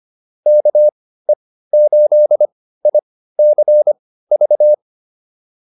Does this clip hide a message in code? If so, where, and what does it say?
Yes: Morse code from 0.5 to 4.7 s, heard as a keyed tone, about 25 wpm, spaced as for 15 wpm, 600 Hz. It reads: KE8ICV